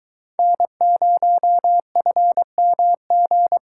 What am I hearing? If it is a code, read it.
Morse code, heard as a keyed tone, about 23 wpm, 700 Hz. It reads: N0FMG